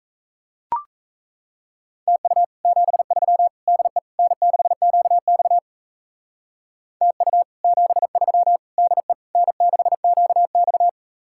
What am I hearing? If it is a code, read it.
Morse code, heard as a keyed tone, about 33 wpm, 700 Hz. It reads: TU 73 DE N6QX TU 73 DE N6QX
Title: Speed-Racing Common QSO Phrases - 28wpm